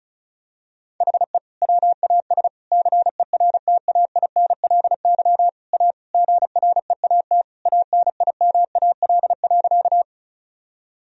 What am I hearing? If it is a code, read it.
Morse code, heard as a keyed tone, about 36 wpm, 700 Hz. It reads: HE WAS CERTAINLY A GREAT ANIMAL.
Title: Sentences from Top 1000 Words - Review Part 1 - 35wpm